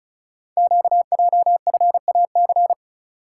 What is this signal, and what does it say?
Morse code, heard as a keyed tone, about 36 wpm, 690 Hz. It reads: QJFAC
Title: Code Groups - 5 Letters - 35wpm